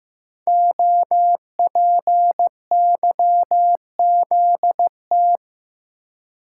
Morse code, heard as a keyed tone, about 15 wpm, 700 Hz. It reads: OPYZT
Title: Code Groups - 5 Letters - 15wpm